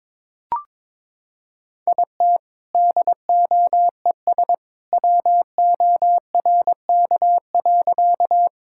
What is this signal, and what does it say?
Morse code, heard as a keyed tone, about 22 wpm, 700 Hz. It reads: IT DOES WORK.
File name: practice / 22wpm / Sentences with 2-Word Phrases - 22wpm